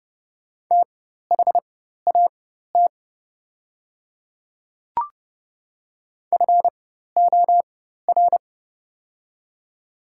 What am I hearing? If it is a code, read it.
Morse code, heard as a keyed tone, about 30 wpm, spaced as for 14 wpm, 700 Hz. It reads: THAT FOR
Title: 4x Character Spacing - Top 100 Words - 30wpm